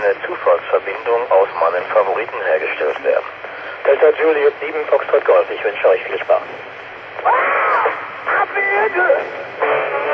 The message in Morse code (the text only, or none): none